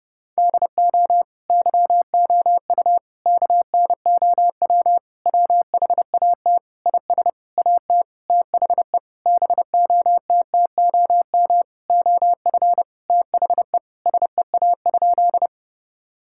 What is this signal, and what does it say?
Morse code, heard as a keyed tone, about 30 wpm, 700 Hz. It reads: DO YOU KNOW WHAT IS AT THE BOTTOM OF THE SEA?
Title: Sentences from Top 1000 Words - Review Part 6 - 30wpm